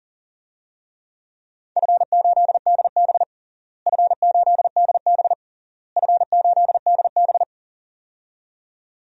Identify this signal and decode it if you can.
Morse code, heard as a keyed tone, about 40 wpm, 700 Hz. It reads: F8DB F8DB F8DB
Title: DXCC-100 Call Signs - 4 Characters - Repeated 3x - 40wpm